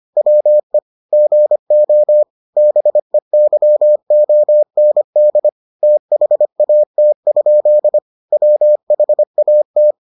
Morse code, heard as a keyed tone, about 25 wpm, 600 Hz. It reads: WE GO BEYOND THAT? WHAT